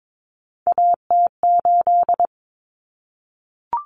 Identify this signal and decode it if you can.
Morse code, heard as a keyed tone, about 22 wpm, 700 Hz. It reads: AT8
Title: Common Cut Numbers - 3-Digit 22wpm